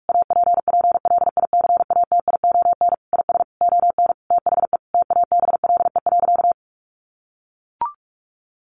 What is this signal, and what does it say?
Morse code, heard as a keyed tone, about 44 wpm, 700 Hz. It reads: APPLICATION IS ON THE TABLE.
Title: Sentences from Top 1200 Words 45wpm